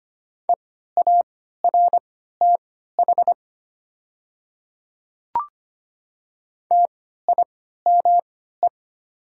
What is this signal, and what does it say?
Morse code, heard as a keyed tone, about 25 wpm, spaced as for 14 wpm, 700 Hz. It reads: EARTH TIME
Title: 3x Character Spacing - Top 200 Words - 25wpm